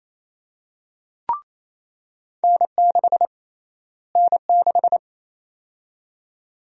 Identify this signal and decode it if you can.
Morse code, heard as a keyed tone, about 28 wpm, 700 Hz. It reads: N6 N6
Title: USA Call Sign Prefixes - Repeated 2x - 28wpm